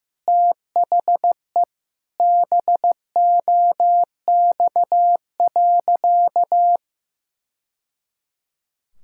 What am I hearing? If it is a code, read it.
Morse code, heard as a keyed tone, about 15 wpm, 700 Hz. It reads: THE BOX.
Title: Sentences from Top 1000 Words - Review Part 6 - 15wpm